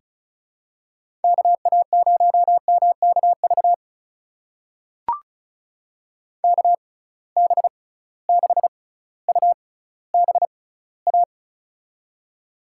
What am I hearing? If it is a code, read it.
Morse code, heard as a keyed tone, about 34 wpm, spaced as for 18 wpm, 700 Hz. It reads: KA0MKV KB6UBA
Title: USA Call Signs - 6 Characters - 6x Character Spacing - 35wpm